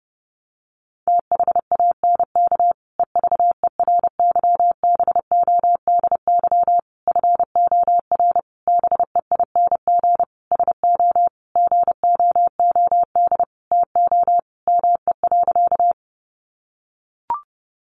Morse code, heard as a keyed tone, about 30 wpm, 700 Hz. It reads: THANK EVERYBODY FOR BEING SO GOOD TO ME.
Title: Sentences from Top 1400 Words 30wpm